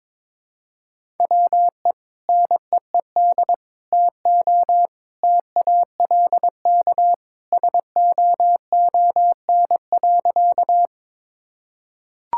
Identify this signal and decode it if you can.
Morse code, heard as a keyed tone, about 22 wpm, 700 Hz. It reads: WE NEED TO TALK SOON.
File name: practice / 22wpm / Sentences with 2-Word Phrases - 22wpm